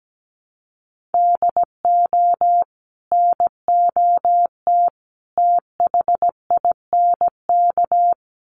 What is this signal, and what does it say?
Morse code, heard as a keyed tone, about 17 wpm, 700 Hz. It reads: DO NOT THINK
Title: Sets of 3 Words 17wpm